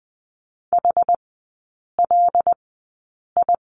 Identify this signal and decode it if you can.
Morse code, heard as a keyed tone, about 20 wpm, spaced as for 8 wpm, 700 Hz. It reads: HLI
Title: 074 Lesson 13e - Letter L - Three Characters 20wpm